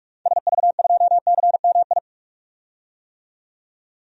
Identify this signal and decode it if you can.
Morse code, heard as a keyed tone, about 46 wpm, 700 Hz. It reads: SV2CMI
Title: DXCC-100 Call Signs - 6 Characters - 45wpm